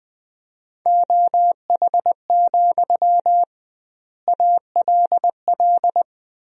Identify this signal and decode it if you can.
Morse code, heard as a keyed tone, about 19 wpm, 700 Hz. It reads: OH, ALL
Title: 2x Word-Spacing - Sentences from Top 500 Words - 20wpm